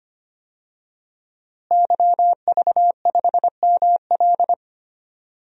Morse code, heard as a keyed tone, about 25 wpm, 700 Hz. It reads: YV5ML
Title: DXCC-100 Call Signs - 5 Characters - Repeated 3x - 25wpm